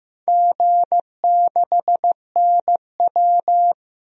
Morse code, heard as a keyed tone, about 15 wpm, 700 Hz. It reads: G6NW
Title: DXCC-100 Call Signs - 4 Characters - 15wpm